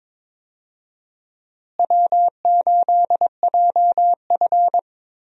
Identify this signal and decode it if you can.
Morse code, heard as a keyed tone, about 22 wpm, 700 Hz. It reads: W8JF
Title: USA Call Signs - 4 Characters - Repeated 2x - 22wpm